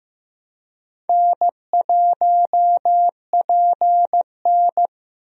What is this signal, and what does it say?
Morse code, heard as a keyed tone, about 15 wpm, 700 Hz. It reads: N1PN
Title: USA Call Signs - 4 Characters - Repeated 2x - 15wpm